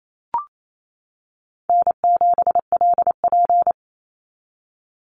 Morse code, heard as a keyed tone, about 28 wpm, 700 Hz. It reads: N7LP
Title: NA SOTA Chaser Callsigns 28wpm